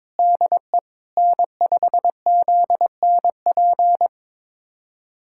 Morse code, heard as a keyed tone, about 22 wpm, 700 Hz. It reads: DE N5ZNP
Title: USA Callsign DE Callsign - 5 Characters - 22wpm